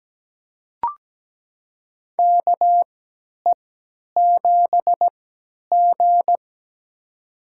Morse code, heard as a keyed tone, about 17 wpm, spaced as for 10 wpm, 700 Hz. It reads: KE7G
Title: USA Call Signs - 4 Characters - 3x Character Spacing - 17wpm